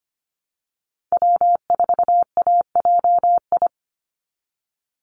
Morse code, heard as a keyed tone, about 25 wpm, 700 Hz. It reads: W4AJI
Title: USA Call Signs 25wpm